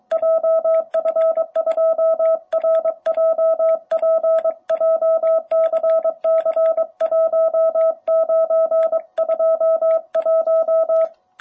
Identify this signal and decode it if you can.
Morse code, heard as a keyed tone, about 23 wpm, 660 Hz. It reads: JF2RJP JCC1921
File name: jh8cbh / 2021cw